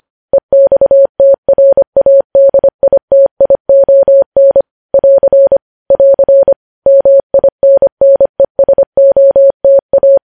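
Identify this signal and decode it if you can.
Morse code, heard as a keyed tone, about 25 wpm, 550 Hz. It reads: EXTRADITION <AR> <AR> MINNESOTA
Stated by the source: News headlines sent in Morse at 25.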